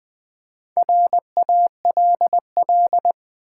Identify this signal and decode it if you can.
Morse code, heard as a keyed tone, about 20 wpm, 700 Hz. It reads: RALL
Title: Build A Word - 1 Letter At A Time - 5-Letter Words - 20wpm